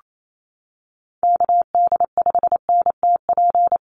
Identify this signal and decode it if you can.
Morse code, heard as a keyed tone, about 28 wpm, 700 Hz. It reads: KD5NTP
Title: USA Call Signs Repeated 2x 28wpm